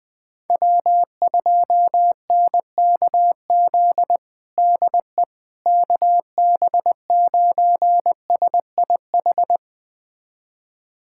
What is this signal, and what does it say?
Morse code, heard as a keyed tone, about 20 wpm, 700 Hz. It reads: W2NKZ DE KB9SIH
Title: USA Callsign DE Callsign 20wpm